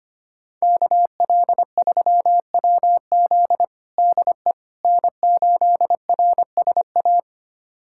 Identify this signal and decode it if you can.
Morse code, heard as a keyed tone, about 25 wpm, 700 Hz. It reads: KL3WZ DE N8RSA
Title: USA Callsign DE Callsign - 5 Characters - 25wpm